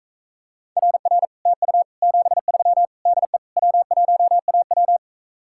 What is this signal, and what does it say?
Morse code, heard as a keyed tone, about 45 wpm, 700 Hz. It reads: RR TU 73 DE W1AW